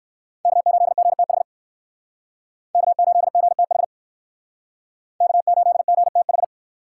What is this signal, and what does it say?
Morse code, heard as a keyed tone, about 58 wpm, 700 Hz. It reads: K8GTH K8GTH K8GTH